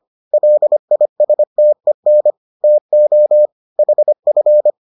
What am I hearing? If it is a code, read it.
Morse code, heard as a keyed tone, about 25 wpm, 600 Hz. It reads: LISTEN TO HF